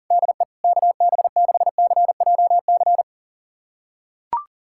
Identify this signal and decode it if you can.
Morse code, heard as a keyed tone, about 40 wpm, 700 Hz. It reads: DE KB6CJC
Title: USA Callsign DE Callsign - 6 Characters - 40wpm